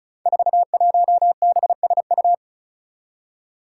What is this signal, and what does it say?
Morse code, heard as a keyed tone, about 36 wpm, 700 Hz. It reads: 41BSU
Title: Five Challenging Letter-Number Combinations 35wpm